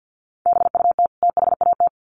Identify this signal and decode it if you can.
Morse code, heard as a keyed tone, about 50 wpm, 700 Hz. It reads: BUT THAT
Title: Sets of 2 Words 50wpm